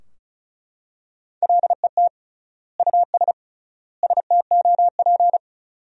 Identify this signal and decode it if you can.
Morse code, heard as a keyed tone, about 36 wpm, 700 Hz. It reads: LET US STOP
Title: 3x Word-Spacing - Sentences from Top 400 Words - 35wpm